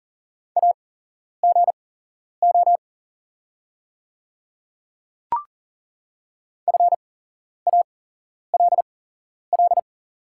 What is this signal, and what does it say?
Morse code, heard as a keyed tone, about 38 wpm, spaced as for 11 wpm, 690 Hz. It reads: AGO FALL
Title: 8x Character Spacing - Top 500 Words - 40wpm